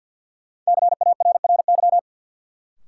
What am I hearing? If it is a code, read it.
Morse code, heard as a keyed tone, about 52 wpm, 700 Hz. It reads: CARRY
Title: Sentences from Top 400 Words - New Word Drill - 50wpm